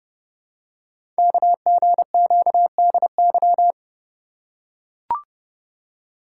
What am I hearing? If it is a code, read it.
Morse code, heard as a keyed tone, about 30 wpm, 700 Hz. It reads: KGQDY
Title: Code Groups - 5 Letters - 30wpm